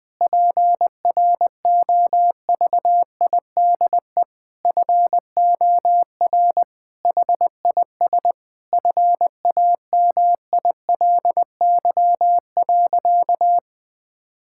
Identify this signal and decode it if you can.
Morse code, heard as a keyed tone, about 21 wpm, 700 Hz. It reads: PROVIDE FOR HIS FAMILY.
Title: Sentences from Top 1000 Words - Review Part 5 - 20wpm